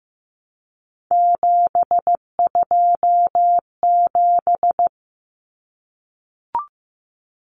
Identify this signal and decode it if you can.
Morse code, heard as a keyed tone, about 15 wpm, 700 Hz. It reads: 727
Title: Numbers - 3-Digit 15wpm